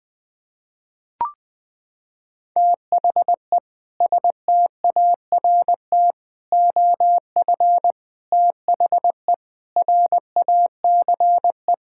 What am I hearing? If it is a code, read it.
Morse code, heard as a keyed tone, about 20 wpm, 700 Hz. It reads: THE START OF THE RACE
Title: Sentences with 4-Word Phrases - 20wpm